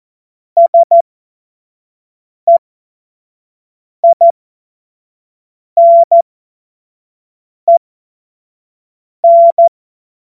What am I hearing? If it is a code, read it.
Morse code, heard as a keyed tone, about 14 wpm, spaced as for 5 wpm, 680 Hz. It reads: SEINEN
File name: Morsekurs / Witze / Witz 15. Juni 2021, 25 F70